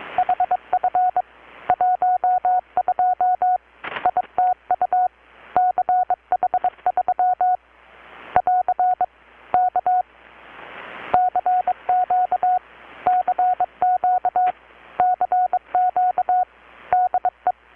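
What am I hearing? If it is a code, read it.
Morse code, heard as a keyed tone, about 22 wpm, 720 Hz. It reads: HF 12 ITU CH3 <AR> K CQ CQ CQ DE
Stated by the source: Morse from KPH's Night of Nights XX maritime radiotelegraphy event, received in Chiapas, Mexico about 3,800 km away on a Kenwood R-600 with a grounded 10 m long-wire antenna.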